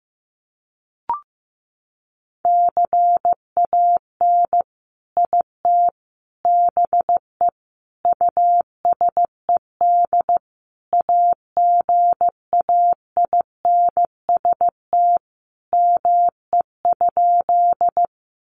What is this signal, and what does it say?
Morse code, heard as a keyed tone, about 15 wpm, 700 Hz. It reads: CAN IT BE USED AGAINST ME?
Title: Sentences from Top 600 Words 15wpm